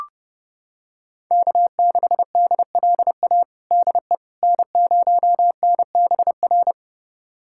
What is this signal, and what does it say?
Morse code, heard as a keyed tone, about 32 wpm, 700 Hz. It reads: K6DLA DE N0NBR
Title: USA Callsign DE Callsign - 5 Characters - 30wpm